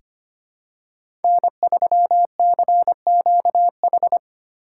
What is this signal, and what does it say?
Morse code, heard as a keyed tone, about 25 wpm, 700 Hz. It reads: N3CQH
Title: USA Call Signs - 5 Characters - 3x Character Spacing - 25wpm